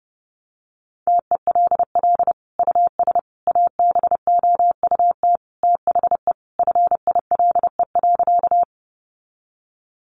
Morse code, heard as a keyed tone, about 30 wpm, 700 Hz. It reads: TELL US ABOUT THE FILE.
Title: Sentences from Top 1200 Words 30wpm